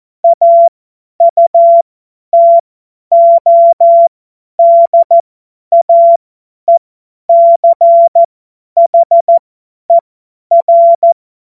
Morse code, heard as a keyed tone, about 14 wpm, spaced as for 10 wpm, 680 Hz. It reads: AUTODAECHER
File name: Morsekurs / Witze / Witz 01. Juni 2021, 50 F70